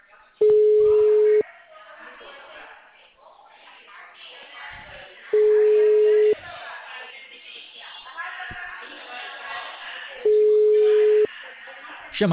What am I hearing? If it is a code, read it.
Ringback tone (Europe, 425 Hz), with speech.
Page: Dracena (Call-Center)